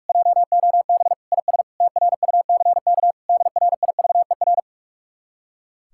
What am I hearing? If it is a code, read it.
Morse code, heard as a keyed tone, about 59 wpm, spaced as for 52 wpm, 700 Hz. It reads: JOB IS TRUCK DRIVER
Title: Speed-Racing Common QSO Phrases - 45wpm